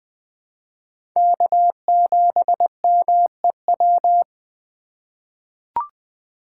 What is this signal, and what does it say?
Morse code, heard as a keyed tone, about 20 wpm, 700 Hz. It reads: K7MEW
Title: USA Call Signs - 5 Characters - 2x Character Spacing - 20wpm